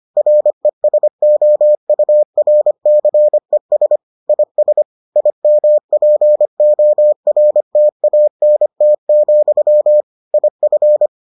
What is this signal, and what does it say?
Morse code, heard as a keyed tone, about 25 wpm, 600 Hz. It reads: RESOURCES IS IMPORTANT, IF